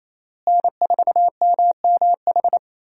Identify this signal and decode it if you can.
Morse code, heard as a keyed tone, about 27 wpm, 700 Hz. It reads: N4MMH
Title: USA Call Signs - 5 Characters - Repeated 3x - 28wpm